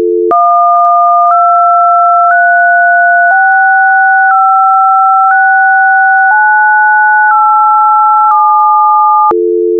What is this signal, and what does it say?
DTMF, heard as telephone touch tones, tones of about 1010 ms. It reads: 123656987